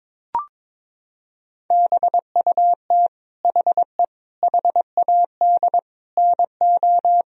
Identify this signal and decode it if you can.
Morse code, heard as a keyed tone, about 22 wpm, 700 Hz. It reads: BUT HE HAD NO
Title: Sentences from Top 1000 Words - Review Part 4 - 22wpm